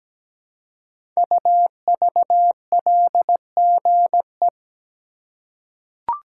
Morse code, heard as a keyed tone, about 17 wpm, 700 Hz. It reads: UVLGE
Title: Code Groups - 5 Letter:Numbers - 17wpm